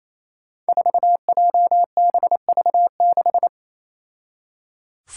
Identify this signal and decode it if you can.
Morse code, heard as a keyed tone, about 28 wpm, 700 Hz. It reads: 4JBV6
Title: Five Challenging Letter-Number Combinations 28wpm